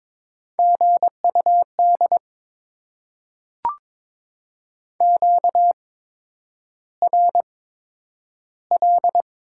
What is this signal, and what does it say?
Morse code, heard as a keyed tone, about 23 wpm, spaced as for 9 wpm, 700 Hz. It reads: GUD QRL
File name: Top 20 QSO Elements - 8x Character Spacing - 22wpm